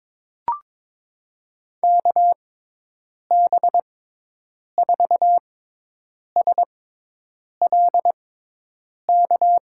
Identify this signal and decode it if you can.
Morse code, heard as a keyed tone, about 22 wpm, spaced as for 8 wpm, 700 Hz. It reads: KB4SLK
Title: USA Call Signs - 6 Characters - 6x Character Spacing - 22wpm